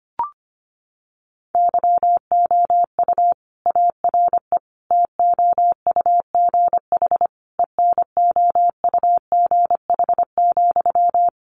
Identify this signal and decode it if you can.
Morse code, heard as a keyed tone, about 25 wpm, 700 Hz. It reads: YOU ARE TOUGH ENOUGH,
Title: Sentences from Top 1200 Words 25wpm